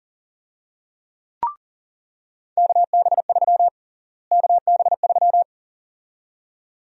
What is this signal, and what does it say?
Morse code, heard as a keyed tone, about 40 wpm, 700 Hz. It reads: KB3 KB3
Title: USA Call Sign Prefixes - Repeated 2x - 40wpm